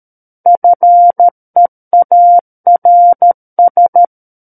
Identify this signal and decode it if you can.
Morse code, heard as a keyed tone, about 13 wpm, 700 Hz. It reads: FEARS
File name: Book One / 04-Chapter 4 - 13 WPM